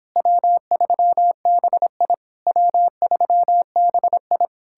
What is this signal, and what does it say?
Morse code, heard as a keyed tone, about 29 wpm, 700 Hz. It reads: W3BI W3BI